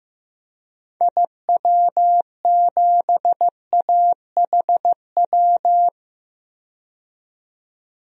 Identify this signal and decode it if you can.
Morse code, heard as a keyed tone, about 15 wpm, 700 Hz. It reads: IW7AHW
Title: DXCC-100 Call Signs - 6 Characters - 15wpm